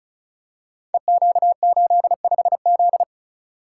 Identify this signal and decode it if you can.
Morse code, heard as a keyed tone, about 36 wpm, 700 Hz. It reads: EQ85Z
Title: Code Groups - 5 Letter:Numbers - 35wpm